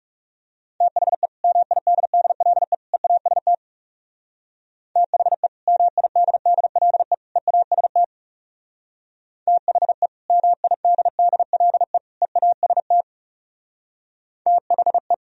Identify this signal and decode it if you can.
Morse code, heard as a keyed tone, about 40 wpm, 700 Hz. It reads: THE MIDDLE EAST THE MIDDLE EAST THE MIDDLE EAST THE